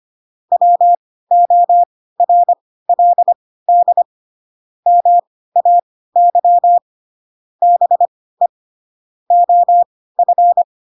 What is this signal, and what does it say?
Morse code, heard as a keyed tone, about 25 wpm, spaced as for 16 wpm, 700 Hz. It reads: WORLD MAY BE OF
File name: Pinocchio / Pinocchio ch7 10 25 ik1hge